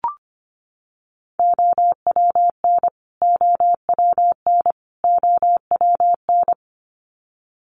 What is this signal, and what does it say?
Morse code, heard as a keyed tone, about 25 wpm, 700 Hz. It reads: OWN OWN OWN
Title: Top 500 Words Repeated 25wpm